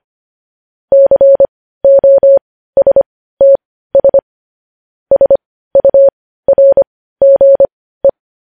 Morse code, heard as a keyed tone, about 25 wpm, spaced as for 15 wpm, 550 Hz. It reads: COSTS SURGE
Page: news-headlines-in-morse-code-at-15-wpm